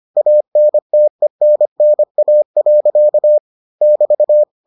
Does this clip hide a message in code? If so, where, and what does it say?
Yes: Morse code from 0.2 to 4.4 s, heard as a keyed tone, about 25 wpm, 600 Hz. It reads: ANTENNA. <BT>